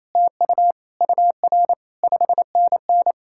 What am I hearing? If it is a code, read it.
Morse code, heard as a keyed tone, about 37 wpm, 700 Hz. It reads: TU UR 5NN